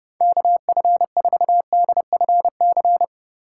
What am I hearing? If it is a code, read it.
Morse code, heard as a keyed tone, about 32 wpm, 700 Hz. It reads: KF4DFC
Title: USA Call Signs - 6 Characters - 4x Character Spacing - 30wpm